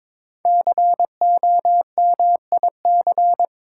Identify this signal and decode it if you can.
Morse code, heard as a keyed tone, about 22 wpm, 700 Hz. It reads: COMIC
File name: 5 Letter Words - First 3 Letters Given - 22wpm